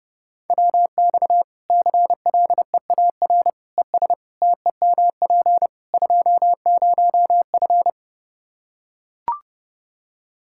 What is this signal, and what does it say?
Morse code, heard as a keyed tone, about 30 wpm, 700 Hz. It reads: WX CLEAR ES TEMP 20F
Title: K1USN SST QSO Practice 30wpm